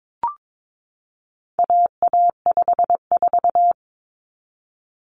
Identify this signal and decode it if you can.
Morse code, heard as a keyed tone, about 22 wpm, 700 Hz. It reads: AA54
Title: Common Cut Numbers - 4-Digit 22wpm